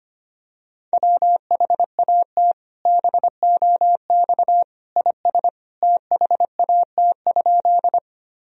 Morse code, heard as a keyed tone, about 25 wpm, 700 Hz. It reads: WHAT BOX IS THAT?
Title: Sentences from Top 1000 Words - Review Part 2 - 25wpm